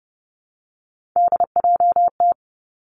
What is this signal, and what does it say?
Morse code, heard as a keyed tone, about 30 wpm, 700 Hz. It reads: DJT
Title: Three Challenging Letter-Number Combinations 30wpm